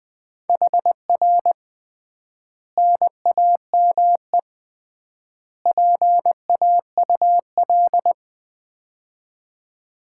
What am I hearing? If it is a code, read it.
Morse code, heard as a keyed tone, about 20 wpm, 700 Hz. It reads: HR NAME PAUL
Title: Common QSO Phrases - 3x Word-Spacing - 20wpm